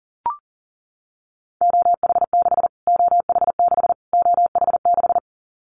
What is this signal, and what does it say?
Morse code, heard as a keyed tone, about 40 wpm, 700 Hz. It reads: OH6 OH6 OH6
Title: Callsign Prefixes for Top 25 DXCC - Repeated 3x - 40wpm